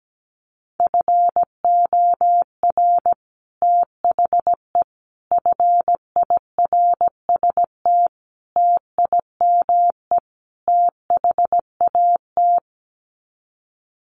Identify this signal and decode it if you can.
Morse code, heard as a keyed tone, about 17 wpm, 700 Hz. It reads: FOR THE FIRST TIME THAT
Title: Sets of 5 Words Encore 17wpm